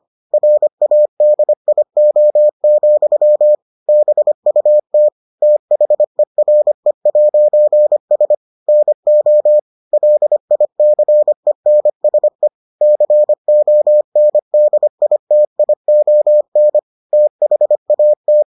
Morse code, heard as a keyed tone, about 25 wpm, 600 Hz. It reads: RADIO, BUT THERE'S NO LICENSE CONDITION THAT